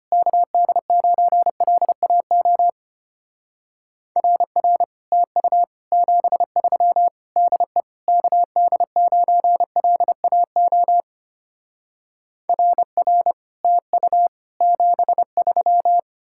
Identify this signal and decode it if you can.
Morse code, heard as a keyed tone, about 33 wpm, 700 Hz. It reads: KD9LAO RR TU 73 DE KD9LAO RR TU 73